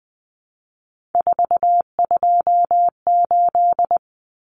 Morse code, heard as a keyed tone, about 20 wpm, 700 Hz. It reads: 428
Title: Common Cut Numbers - 3-Digit 20wpm